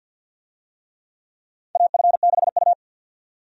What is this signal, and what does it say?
Morse code, heard as a keyed tone, about 52 wpm, 700 Hz. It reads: AF6U
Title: USA Call Signs - 4 Characters - 50wpm